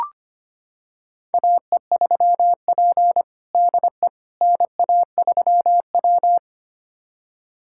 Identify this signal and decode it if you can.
Morse code, heard as a keyed tone, about 25 wpm, 700 Hz. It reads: AE3P DE NA3W
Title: USA Callsign DE Callsign - 4 Characters - 25wpm